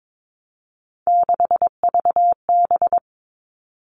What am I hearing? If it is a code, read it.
Morse code, heard as a keyed tone, about 22 wpm, 700 Hz. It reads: 6VB
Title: All Cut Numbers - 3-Digit 22wpm